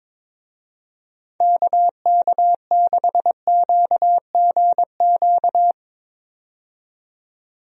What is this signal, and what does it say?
Morse code, heard as a keyed tone, about 22 wpm, 700 Hz. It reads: KK6QGQ
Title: USA Call Signs - 6 Characters - 22wpm